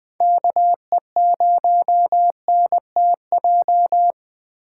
Morse code, heard as a keyed tone, about 20 wpm, 700 Hz. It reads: KE0NTJ
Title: Speed Racing - USA 2x3 Call Signs 17wpm